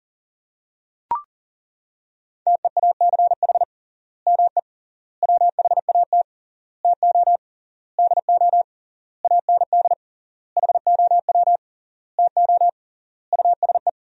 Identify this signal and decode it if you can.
Morse code, heard as a keyed tone, about 37 wpm, 700 Hz. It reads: TEACH ME WHAT TO DO AND HOW TO USE
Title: 3x Word-Spacing - Sentences from Top 500 Words - 40wpm